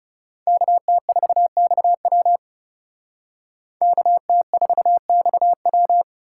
Morse code, heard as a keyed tone, about 38 wpm, 700 Hz. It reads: KT4XW KT4XW